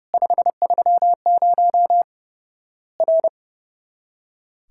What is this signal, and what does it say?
Morse code, heard as a keyed tone, about 30 wpm, 700 Hz. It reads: 530 R
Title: Worked All Europe DX Contest - QTC Exchange - 30wpm